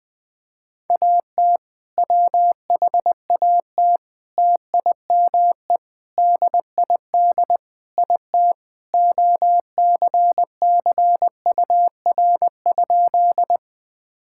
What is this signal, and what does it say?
Morse code, heard as a keyed tone, about 20 wpm, 700 Hz. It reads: AT WHAT TIME DID IT OCCUR?
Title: Sentences from Top 1000 Words - Review Part 6 - 20wpm